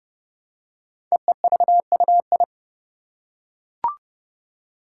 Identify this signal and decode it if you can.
Morse code, heard as a keyed tone, about 30 wpm, 690 Hz. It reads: EEVUI
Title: Code Groups - 5 Letters - 30wpm